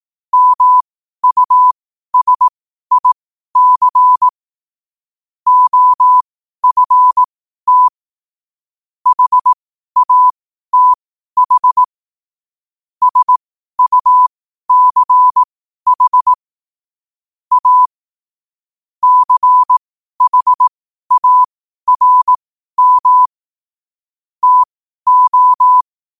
Morse code, heard as a keyed tone, about 18 wpm, spaced as for 12 wpm, 1000 Hz. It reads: MUSIC OFT HATH SUCH A CHARM TO
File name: qotdfiles / 12wpm